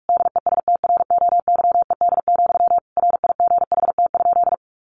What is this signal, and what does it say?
Morse code, heard as a keyed tone, about 44 wpm, 700 Hz. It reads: DESTROYED, RIGHT?
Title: A Sound Of Thunder 45wpm